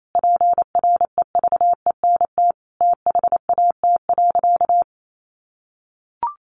Morse code, heard as a keyed tone, about 28 wpm, 700 Hz. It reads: PREVENT THAT.
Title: Sentences from Top 1100 Words 28wpm